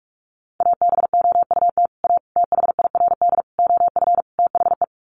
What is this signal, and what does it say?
Morse code, heard as a keyed tone, about 44 wpm, 700 Hz. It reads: ABOUT A THIRD OF THE
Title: Sets of 5 Words Encore2 45wpm